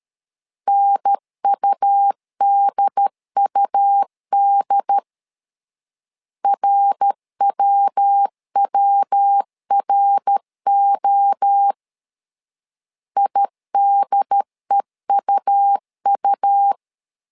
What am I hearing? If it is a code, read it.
Morse code, heard as a keyed tone, about 13 wpm, 800 Hz. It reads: NUDUD RWWRO IDEUU
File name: LEZIONI / 3 MWRO / 3 MWRO 12wpm 1730